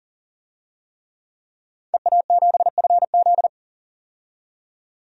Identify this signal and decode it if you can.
Morse code, heard as a keyed tone, about 40 wpm, 700 Hz. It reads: EA7FZ
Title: DXCC-100 Call Signs - 5 Characters - 40wpm